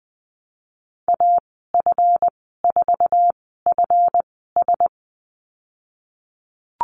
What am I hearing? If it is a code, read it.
Morse code, heard as a keyed tone, about 20 wpm, spaced as for 15 wpm, 700 Hz. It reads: AF4FS
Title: 154 Lesson 25g - Letter p - Call Signs 20wpm